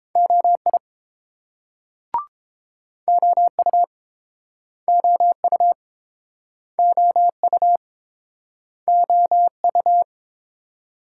Morse code, heard as a keyed tone, about 30 wpm, 700 Hz. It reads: OI OU OU OU OU